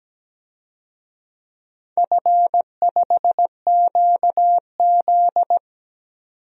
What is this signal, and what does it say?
Morse code, heard as a keyed tone, about 17 wpm, 700 Hz. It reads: F5QZ
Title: DXCC-100 Call Signs - 4 Characters - 17wpm